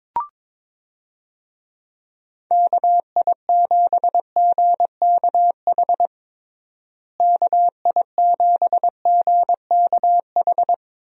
Morse code, heard as a keyed tone, about 22 wpm, 700 Hz. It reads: KI7GKH KI7GKH
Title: USA Call Signs - 6 Characters - Repeated 2x - 22wpm